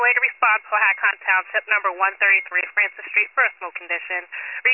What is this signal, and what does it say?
Emergency Alert System (SAME) tones, heard as harsh buzzing bursts, with speech. It is heard from 1.0 to 4.3 s.